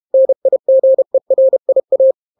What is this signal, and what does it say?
Morse code, heard as a keyed tone, about 32 wpm, 530 Hz. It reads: NIGERIA